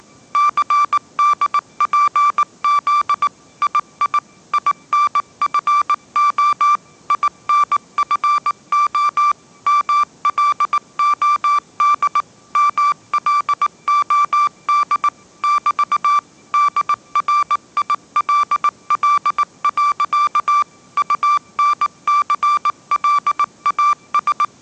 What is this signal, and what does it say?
Morse code, heard as a keyed tone, about 19 wpm, 1170 Hz. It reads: CDPZ II INFO INFO MLOD MLOD <BT> DRILL. UNCLAS
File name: 20wpm